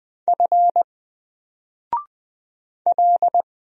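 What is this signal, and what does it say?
Morse code, heard as a keyed tone, about 20 wpm, 700 Hz. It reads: F L
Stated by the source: Single letter and number Morse drill at 20 wpm, from N4LCD.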